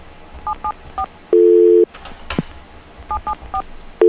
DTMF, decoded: 774 774